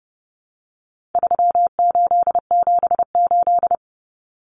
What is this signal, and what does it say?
Morse code, heard as a keyed tone, about 30 wpm, 700 Hz. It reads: 3878
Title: Numbers - 4-Digit 30wpm